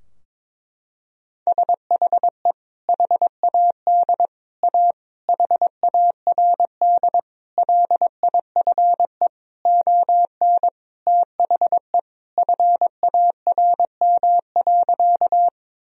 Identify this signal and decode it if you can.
Morse code, heard as a keyed tone, about 22 wpm, 700 Hz. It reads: SHE HAD A HARD LIFE ON THE FARM.
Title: Sentences from Top 1000 Words - Review Part 2 - 22wpm